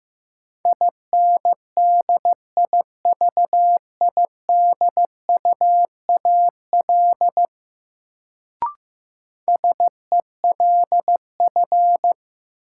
Morse code, heard as a keyed tone, about 15 wpm, 700 Hz. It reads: INDIVIDUAL SELF
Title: Sentences from Top 1400 Words - New Word Drill Alt - 15wpm